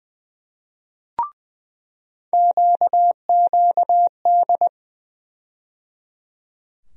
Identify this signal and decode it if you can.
Morse code, heard as a keyed tone, about 20 wpm, 700 Hz. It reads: QQD
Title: USA Call Sign Postfixes - 3 Letters 20wpm